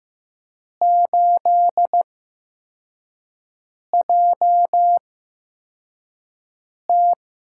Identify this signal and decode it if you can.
Morse code, heard as a keyed tone, about 15 wpm, spaced as for 4 wpm, 700 Hz. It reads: 8JT
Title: USA Call Signs - 4 Characters - 8x Character Spacing - 15wpm